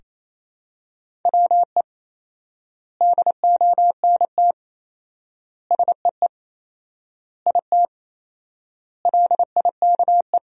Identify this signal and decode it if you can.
Morse code, heard as a keyed tone, about 28 wpm, 700 Hz. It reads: WE DONT SEE IT LIKE
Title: 4x Word-Spacing - Sentences from Top 300 Words - 28wpm